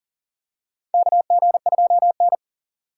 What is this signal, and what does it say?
Morse code, heard as a keyed tone, about 40 wpm, 700 Hz. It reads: KG2N
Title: USA Call Signs - 4 Characters - 6x Character Spacing - 40wpm